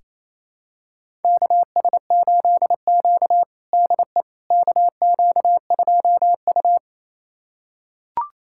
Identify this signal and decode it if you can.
Morse code, heard as a keyed tone, about 28 wpm, 700 Hz. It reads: KS8Q DE KQ2U
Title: USA Callsign DE Callsign - 4 Characters - 28wpm